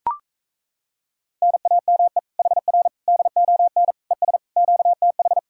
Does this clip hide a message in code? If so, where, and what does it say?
Yes: Morse code from 1.4 to 5.4 s, heard as a keyed tone, about 38 wpm, 700 Hz. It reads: NAME HR DON ES QTH